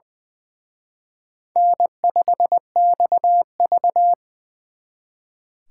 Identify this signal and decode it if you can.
Morse code, heard as a keyed tone, about 23 wpm, 700 Hz. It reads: N5XV
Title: Speed Racing - USA Call Signs 20wpm